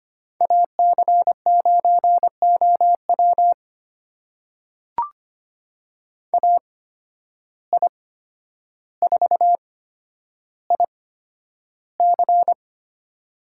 Morse code, heard as a keyed tone, about 25 wpm, spaced as for 11 wpm, 700 Hz. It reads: AC9OW AI4IC
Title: USA Call Signs - 5 Characters - 8x Character Spacing - 25wpm